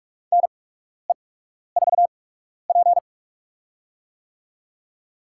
Morse code, heard as a keyed tone, about 47 wpm, spaced as for 12 wpm, 690 Hz. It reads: NE4P
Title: USA Call Signs - 8x Character Spacing - 45wpm